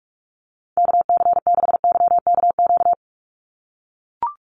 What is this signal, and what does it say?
Morse code, heard as a keyed tone, about 44 wpm, 700 Hz. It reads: KC6YKQ
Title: USA Call Signs 45wpm